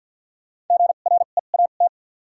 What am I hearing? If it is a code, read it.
Morse code, heard as a keyed tone, about 51 wpm, spaced as for 36 wpm, 700 Hz. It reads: GREAT
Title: 2x Character Spacing - Top 300 Words - 50wpm